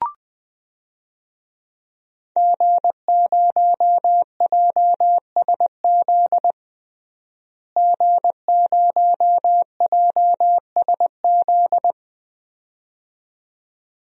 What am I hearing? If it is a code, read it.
Morse code, heard as a keyed tone, about 21 wpm, 700 Hz. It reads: G0JSZ G0JSZ
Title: DXCC-100 Call Signs - 5 Characters - Repeated 2x - 20wpm